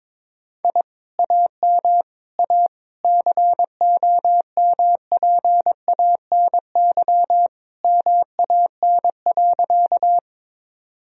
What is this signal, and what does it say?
Morse code, heard as a keyed tone, about 22 wpm, 700 Hz. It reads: I AM A COMPANY MAN.
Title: Sentences from Top 1000 Words - Review Part 3 - 22wpm